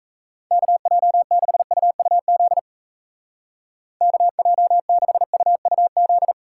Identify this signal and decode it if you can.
Morse code, heard as a keyed tone, about 37 wpm, 700 Hz. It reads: KJ6UUZ KJ6UUZ